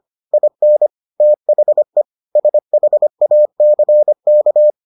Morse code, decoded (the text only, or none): IN THE SHACK